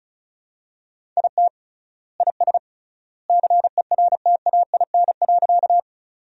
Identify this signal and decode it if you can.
Morse code, heard as a keyed tone, about 34 wpm, 700 Hz. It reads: IT IS CERTAIN.
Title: 3x Word-Spacing - Sentences from Top 500 Words - 35wpm